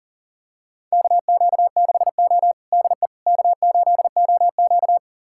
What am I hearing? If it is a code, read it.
Morse code, heard as a keyed tone, about 40 wpm, 700 Hz. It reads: KQ6O DE K8OQ
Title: USA Callsign DE Callsign - 4 Characters - 40wpm